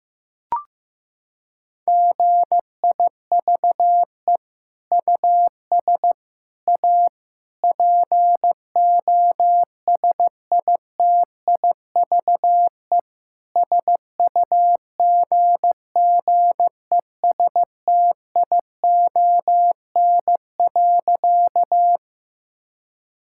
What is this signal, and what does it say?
Morse code, heard as a keyed tone, about 15 wpm, 700 Hz. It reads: GIVE US A POSITIVE SUGGESTION.
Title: Sentences from Top 1600 Words 15wpm